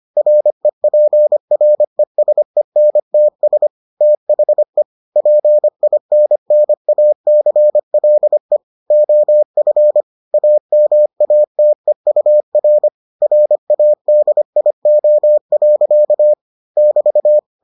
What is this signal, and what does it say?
Morse code, heard as a keyed tone, about 25 wpm, 600 Hz. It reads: REPRESENTS THE PINNACLE OF AMATEUR RADIO. <BT>